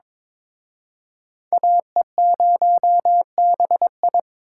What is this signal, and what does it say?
Morse code, heard as a keyed tone, about 26 wpm, 700 Hz. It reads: AE0BI
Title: Speed Racing - USA 2x2 Call Signs 22wpm